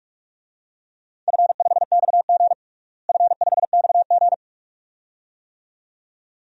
Morse code, heard as a keyed tone, about 46 wpm, 700 Hz. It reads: F5XG F5XG
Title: DXCC-100 Call Signs - 4 Characters - Repeated 2x - 45wpm